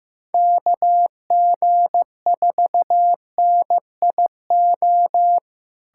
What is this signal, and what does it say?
Morse code, heard as a keyed tone, about 15 wpm, 700 Hz. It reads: KG4NIO
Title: USA Callsign DE Callsign - 6 Characters - 15wpm